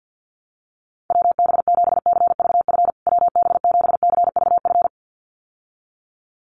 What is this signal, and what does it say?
Morse code, heard as a keyed tone, about 50 wpm, 700 Hz. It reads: WB7CVF WB7CVF
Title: USA Call Signs Repeated 2x 50wpm